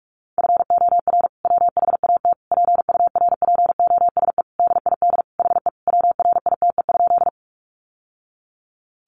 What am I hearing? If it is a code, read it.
Morse code, heard as a keyed tone, about 44 wpm, 700 Hz. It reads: FOR WHAT PURPOSE DID HE WRITE?
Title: Sentences from Top 800 Words 45wpm